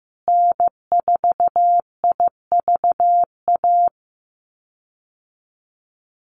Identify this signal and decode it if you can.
Morse code, heard as a keyed tone, about 15 wpm, 700 Hz. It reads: N4IVA
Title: USA Call Signs Repeated 2x 15wpm